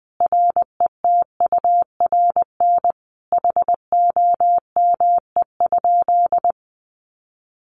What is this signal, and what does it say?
Morse code, heard as a keyed tone, about 20 wpm, 700 Hz. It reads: RETURN HOME?
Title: Sentences from Top 800 Words 20wpm